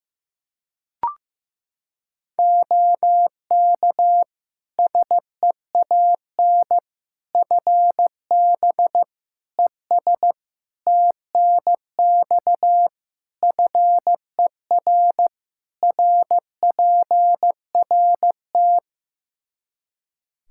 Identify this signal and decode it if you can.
Morse code, heard as a keyed tone, about 15 wpm, 700 Hz. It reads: OK SEAN FB ES TNX FER RPRT
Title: Practice QSO - Exchange 2 - Info - 15wpm